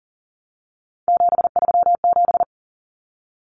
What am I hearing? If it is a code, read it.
Morse code, heard as a keyed tone, about 40 wpm, 700 Hz. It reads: 737
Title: Numbers - 3-Digit 40wpm